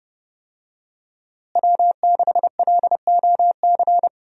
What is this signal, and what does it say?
Morse code, heard as a keyed tone, about 30 wpm, 700 Hz. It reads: W6LOC
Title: USA Call Signs - 5 Characters - Repeated 2x - 30wpm